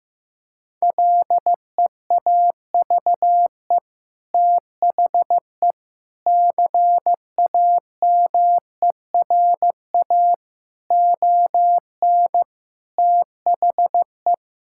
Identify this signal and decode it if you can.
Morse code, heard as a keyed tone, about 16 wpm, 700 Hz. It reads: LEAVE THE CAMERA ON THE
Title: Sentences from Top 1000 Words - Review Part 5 - 15wpm